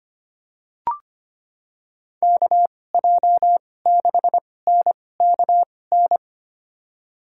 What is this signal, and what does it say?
Morse code, heard as a keyed tone, about 25 wpm, spaced as for 18 wpm, 700 Hz. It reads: KJ6NKN
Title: USA Call Signs - 2x Character Spacing - 25wpm